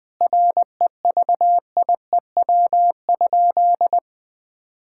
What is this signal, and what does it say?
Morse code, heard as a keyed tone, about 20 wpm, 700 Hz. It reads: REVIEW?
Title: Sentences from Top 1000 Words - Review Part 6 - 20wpm